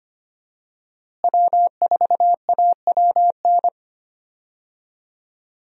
Morse code, heard as a keyed tone, about 25 wpm, 700 Hz. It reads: W4AWN
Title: USA Call Signs - 5 Characters - Repeated 3x - 25wpm